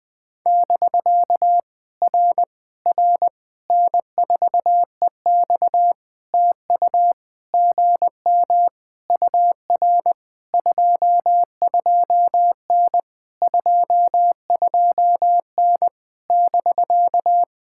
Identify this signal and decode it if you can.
Morse code, heard as a keyed tone, about 20 wpm, 700 Hz. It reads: <BK> R R N4EX TU GM UR 22N 22N <BK>
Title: POTA QSO Practice - 20wpm